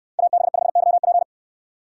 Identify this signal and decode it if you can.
Morse code, heard as a keyed tone, about 68 wpm, 690 Hz. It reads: AB5YP